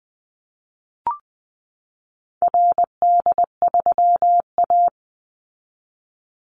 Morse code, heard as a keyed tone, about 20 wpm, 700 Hz. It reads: RD3A
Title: Call Signs Repeated 2x - 20wpm